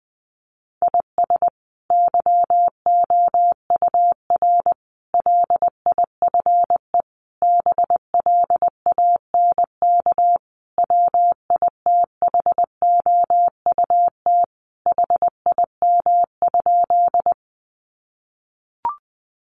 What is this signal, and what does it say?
Morse code, heard as a keyed tone, about 20 wpm, 700 Hz. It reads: IS YOUR LIFE BLANK WITHOUT HIM?
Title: Sentences from Top 1500 Words 20wpm